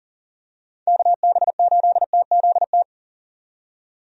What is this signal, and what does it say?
Morse code, heard as a keyed tone, about 40 wpm, 700 Hz. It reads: KB8TZT
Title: USA Call Signs - 6 Characters - 40wpm